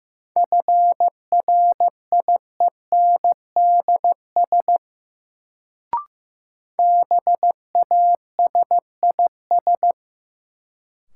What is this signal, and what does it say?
Morse code, heard as a keyed tone, about 15 wpm, 700 Hz. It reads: FRIENDS BASIS
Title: Sentences from Top 1000 Words - Review Part 5 - New Word Drill - 15wpm